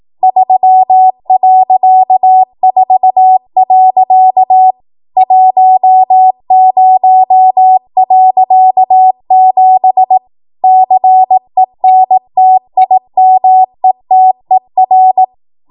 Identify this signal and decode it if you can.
Morse code, heard as a keyed tone, about 18 wpm, 750 Hz. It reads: 3.4. 10.7 CENTIMETER
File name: Morse / 18WPM